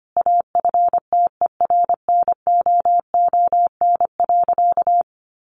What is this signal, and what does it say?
Morse code, heard as a keyed tone, about 25 wpm, 700 Hz. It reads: AFTERNOON.
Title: Sentences from Top 1500 Words 25wpm